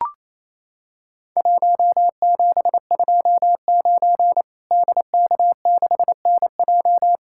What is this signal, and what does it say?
Morse code, heard as a keyed tone, about 28 wpm, 700 Hz. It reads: 1729 DK6NJ
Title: Worked All Europe DX Contest - QTC Exchange - 28wpm